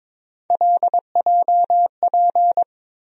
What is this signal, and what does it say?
Morse code, heard as a keyed tone, about 22 wpm, 700 Hz. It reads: LJP
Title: USA Call Sign Postfixes - 3 Letters 22wpm